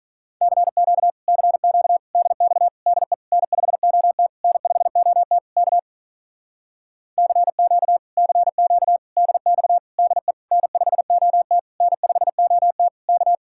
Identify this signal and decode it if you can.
Morse code, heard as a keyed tone, about 46 wpm, 700 Hz. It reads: CQ CQ DX DE N5OT N5OT K CQ CQ DX DE N5OT N5OT K